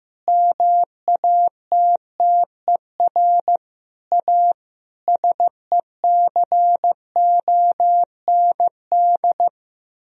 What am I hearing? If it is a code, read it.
Morse code, heard as a keyed tone, about 15 wpm, 700 Hz. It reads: MATTER A SECOND
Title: Sentences from Top 1000 Words - Review Part 2 - 15wpm